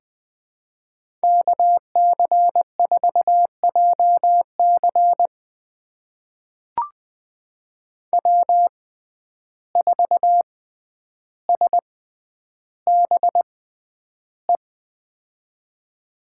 Morse code, heard as a keyed tone, about 20 wpm, spaced as for 10 wpm, 700 Hz. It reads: KC4JC W4SBE
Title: USA Call Signs - 5 Characters - 6x Character Spacing - 20wpm